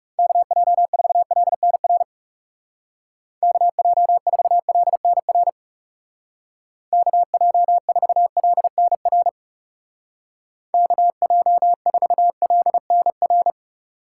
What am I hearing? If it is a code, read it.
Morse code, heard as a keyed tone, about 36 wpm, 700 Hz. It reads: KJ4LNR KJ4LNR KJ4LNR KJ4LNR
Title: Speed Racing - USA 2x3 Call Signs 30wpm